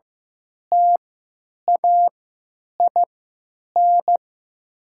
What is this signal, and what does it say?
Morse code, heard as a keyed tone, about 15 wpm, spaced as for 9 wpm, 700 Hz. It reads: TAIN
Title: 3x Character Spacing - Top 500 Words - 15wpm